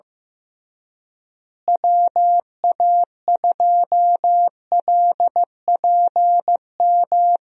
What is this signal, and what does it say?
Morse code, heard as a keyed tone, about 15 wpm, 700 Hz. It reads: WA2LPM
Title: USA Call Signs - 6 Characters - Repeated 3x - 15wpm